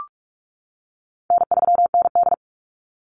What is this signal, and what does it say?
Morse code, heard as a keyed tone, about 46 wpm, 700 Hz. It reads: N3ND
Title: Easy Call Signs - 45wpm